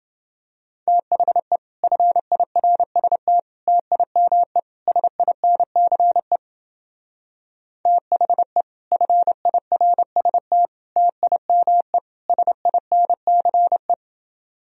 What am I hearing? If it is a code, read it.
Morse code, heard as a keyed tone, about 28 wpm, 700 Hz. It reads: THE FIRST TIME SINCE THE FIRST TIME SINCE